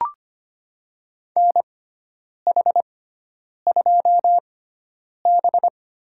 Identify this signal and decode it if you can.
Morse code, heard as a keyed tone, about 25 wpm, spaced as for 9 wpm, 700 Hz. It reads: NH2B
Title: USA Call Signs - 6x Character Spacing - 25wpm